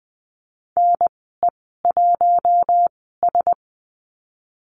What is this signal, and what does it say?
Morse code, heard as a keyed tone, about 20 wpm, spaced as for 14 wpm, 700 Hz. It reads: NE1S
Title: 042 Lesson 8g - Number 1 - Call Signs 20wpm